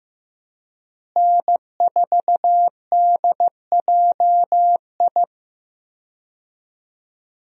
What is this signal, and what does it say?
Morse code, heard as a keyed tone, about 15 wpm, 700 Hz. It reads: N4DJI
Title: USA Call Signs - 5 Characters - Repeated 3x - 15wpm